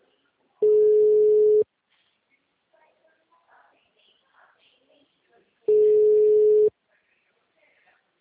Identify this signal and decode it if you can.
Ringback tone (Europe, 425 Hz), with speech.